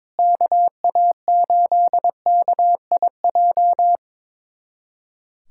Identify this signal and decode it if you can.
Morse code, heard as a keyed tone, about 22 wpm, 700 Hz. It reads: KA8KIJ
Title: USA Callsign DE Callsign - 6 Characters - 22wpm